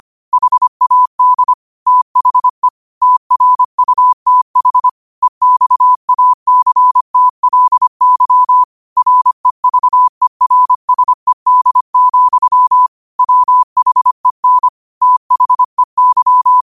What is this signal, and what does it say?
Morse code, heard as a keyed tone, about 25 wpm, 1000 Hz. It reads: HAD THE TRUTH EXACTLY REVERSED, WHEN THEY
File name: qotdfiles / 25wpm